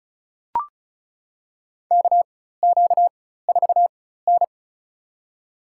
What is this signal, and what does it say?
Morse code, heard as a keyed tone, about 36 wpm, spaced as for 16 wpm, 700 Hz. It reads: KQ4N
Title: USA Call Signs - 4 Characters - 4x Character Spacing - 35wpm